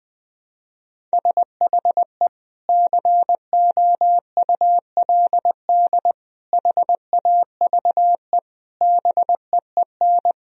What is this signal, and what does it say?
Morse code, heard as a keyed tone, about 20 wpm, 700 Hz. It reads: SHE COULD HAVE BEEN
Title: Sentences from Top 1000 Words - Review Part 1 - 20wpm